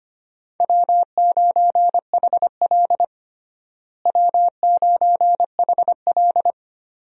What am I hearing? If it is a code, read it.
Morse code, heard as a keyed tone, about 25 wpm, 700 Hz. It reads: W9HL W9HL
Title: USA Call Signs - 4 Characters - Repeated 2x - 25wpm